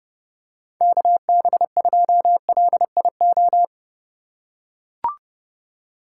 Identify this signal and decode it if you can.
Morse code, heard as a keyed tone, about 30 wpm, 700 Hz. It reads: KB2LIO
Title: USA Call Signs - 3x Character Spacing - 30wpm